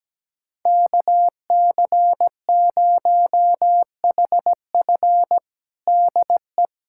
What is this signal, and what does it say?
Morse code, heard as a keyed tone, about 17 wpm, 700 Hz. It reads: KC0HF DE
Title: USA Callsign DE Callsign - 5 Characters - 17wpm